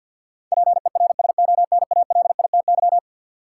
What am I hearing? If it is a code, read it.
Morse code, heard as a keyed tone, about 48 wpm, 700 Hz. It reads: PERSONALITY
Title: Sentences from Top 1400 Words - New Word Drill Alt - 50wpm